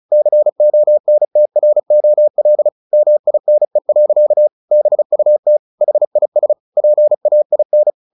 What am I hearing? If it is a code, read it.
Morse code, heard as a keyed tone, about 34 wpm, 600 Hz. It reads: CONTROL MINE. BUT HIS PAIN
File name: the old man and the sea 2 35wpm